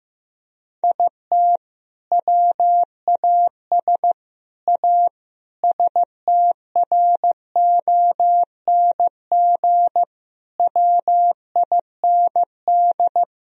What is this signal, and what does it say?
Morse code, heard as a keyed tone, about 16 wpm, 700 Hz. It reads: IT WAS A STRONG WIND
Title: Sentences from Top 1000 Words - Review Part 2 - 15wpm